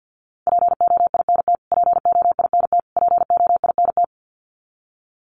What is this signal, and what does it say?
Morse code, heard as a keyed tone, about 50 wpm, 700 Hz. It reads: POINT POINT POINT
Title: Top 200 Words Repeated 50wpm